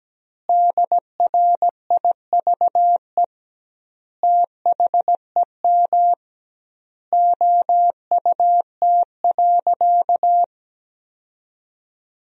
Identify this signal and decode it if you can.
Morse code, heard as a keyed tone, about 17 wpm, 700 Hz. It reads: DRIVE THEM OUT.
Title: 2x Word-Spacing - Sentences from Top 500 Words - 17wpm